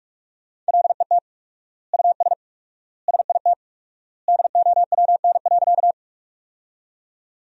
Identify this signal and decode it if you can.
Morse code, heard as a keyed tone, about 46 wpm, 700 Hz. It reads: LET US SIT DOWN.
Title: 4x Word-Spacing - Sentences from Top 500 Words - 45wpm